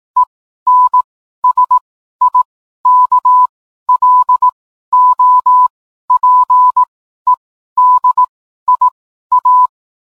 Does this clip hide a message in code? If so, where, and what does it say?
Yes: Morse code from 0.2 to 9.7 s, heard as a keyed tone, about 18 wpm, spaced as for 12 wpm, 1000 Hz. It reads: ENSIKLOPEDIA